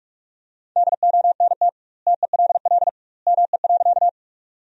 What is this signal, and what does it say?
Morse code, heard as a keyed tone, about 46 wpm, 700 Hz. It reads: DONT TELL ME.
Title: 2x Word-Spacing - Sentences from Top 300 Words - 45wpm